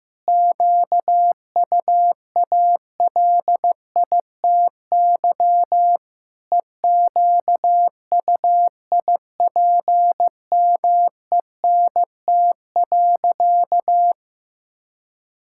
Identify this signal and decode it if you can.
Morse code, heard as a keyed tone, about 15 wpm, 700 Hz. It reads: QUALITY EQUIPMENT.
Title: Sentences from Top 1000 Words - Review Part 6 - 15wpm